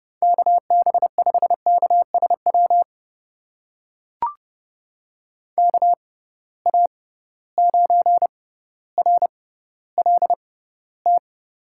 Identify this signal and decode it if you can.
Morse code, heard as a keyed tone, about 31 wpm, spaced as for 16 wpm, 700 Hz. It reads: KB5KSW KA9RLT
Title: USA Call Signs - 6 Characters - 6x Character Spacing - 30wpm